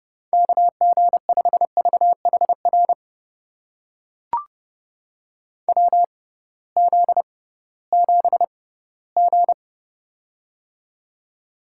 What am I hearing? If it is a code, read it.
Morse code, heard as a keyed tone, about 31 wpm, spaced as for 18 wpm, 700 Hz. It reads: KG5VHR WZ7G
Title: USA Call Signs - 6x Character Spacing - 30wpm